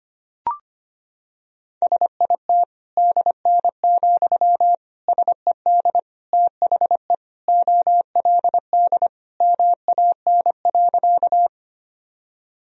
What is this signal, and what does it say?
Morse code, heard as a keyed tone, about 25 wpm, 700 Hz. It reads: SIT DN, SED THE OLD MAN.
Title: Sentences from Top 500 Words and Abbreviations 25wpm